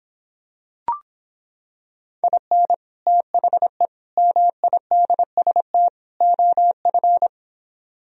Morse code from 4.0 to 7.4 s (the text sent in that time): MIDST OF